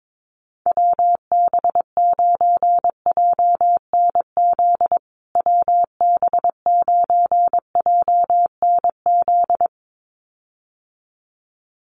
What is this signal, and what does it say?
Morse code, heard as a keyed tone, about 22 wpm, 700 Hz. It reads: WB9JNZ WB9JNZ
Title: USA Call Signs Repeated 2x 22wpm